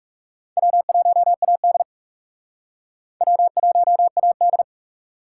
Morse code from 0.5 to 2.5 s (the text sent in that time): W1AD